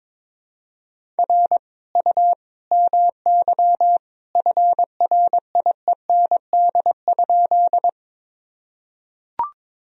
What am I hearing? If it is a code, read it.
Morse code, heard as a keyed tone, about 22 wpm, 700 Hz. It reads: R U MY FRIEND?
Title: Sentences from Top 300 Words and Abbreviations 22wpm